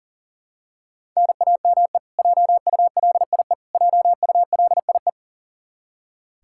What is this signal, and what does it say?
Morse code, heard as a keyed tone, about 40 wpm, 700 Hz. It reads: NAME JULIE JULIE
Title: Practice QSO - Exchange 1 - Beginning - 40wpm